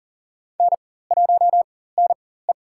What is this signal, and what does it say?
Morse code, heard as a keyed tone, about 40 wpm, spaced as for 19 wpm, 700 Hz. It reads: N1NE
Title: USA Call Signs - 4 Characters - 4x Character Spacing - 40wpm